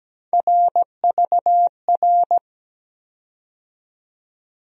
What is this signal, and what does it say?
Morse code, heard as a keyed tone, about 17 wpm, 700 Hz. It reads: RVR
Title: USA Call Sign Postfixes - 3 Letters 17wpm